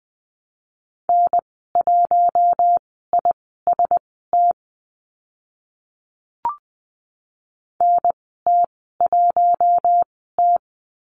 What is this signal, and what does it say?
Morse code, heard as a keyed tone, about 20 wpm, spaced as for 14 wpm, 700 Hz. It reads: N1IST NT1T
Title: 042 Lesson 8g - Number 1 - Call Signs 20wpm